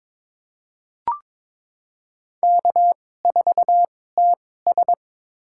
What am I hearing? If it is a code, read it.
Morse code, heard as a keyed tone, about 22 wpm, spaced as for 16 wpm, 700 Hz. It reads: K4TS
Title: USA Call Signs - 4 Characters - 2x Character Spacing - 22wpm